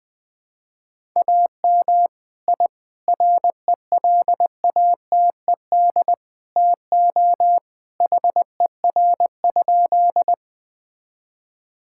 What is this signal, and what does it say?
Morse code, heard as a keyed tone, about 21 wpm, 700 Hz. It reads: AM I RELATED TO HER?
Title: Sentences from Top 1000 Words - Review Part 5 - 20wpm